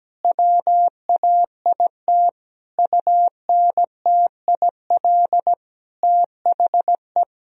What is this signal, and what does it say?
Morse code, heard as a keyed tone, about 17 wpm, 700 Hz. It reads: WAIT UNTIL THE
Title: Sentences from Top 1000 Words - Review Part 2 - 17wpm